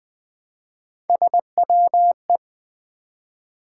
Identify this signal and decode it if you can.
Morse code, heard as a keyed tone, about 20 wpm, 700 Hz. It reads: SWE
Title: Build A Word - 1 Letter At A Time - 3 to 5-Letter Words - 20wpm